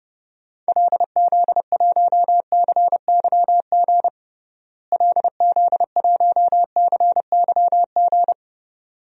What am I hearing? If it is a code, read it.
Morse code, heard as a keyed tone, about 32 wpm, 700 Hz. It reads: LZ1CYG LZ1CYG
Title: DXCC-100 Call Signs - 6 Characters - Repeated 2x - 30wpm